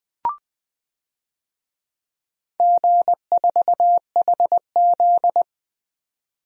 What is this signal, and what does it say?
Morse code, heard as a keyed tone, about 20 wpm, 700 Hz. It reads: G4HZ
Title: DXCC-100 Call Signs - 4 Characters - 20wpm